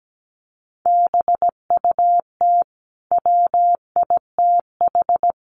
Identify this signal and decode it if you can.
Morse code, heard as a keyed tone, about 17 wpm, 700 Hz. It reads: BUT WITH
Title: Sentences from Top 1500 Words 17wpm